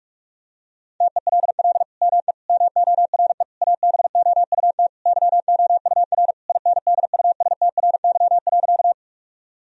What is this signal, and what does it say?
Morse code, heard as a keyed tone, about 41 wpm, 700 Hz. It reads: TELL ME MORE ABOUT YOUR INDUSTRY.
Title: Sentences from Top 1000 Words - Review Part 4 - 45wpm